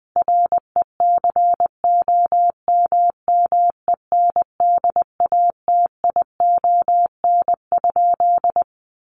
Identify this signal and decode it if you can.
Morse code, heard as a keyed tone, about 20 wpm, 700 Hz. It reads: RECOMMENDATION?
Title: Sentences from Top 1400 Words 20wpm